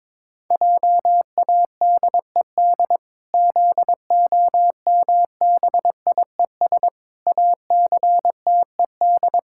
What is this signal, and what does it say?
Morse code, heard as a keyed tone, about 22 wpm, 700 Hz. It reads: JADED ZOMBIES ACTED
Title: Pangram Challenge v2 - 22wpm